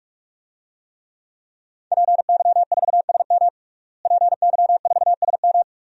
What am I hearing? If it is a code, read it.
Morse code, heard as a keyed tone, about 46 wpm, 700 Hz. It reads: PY4SM PY4SM
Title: DXCC-100 Call Signs - 5 Characters - Repeated 3x - 45wpm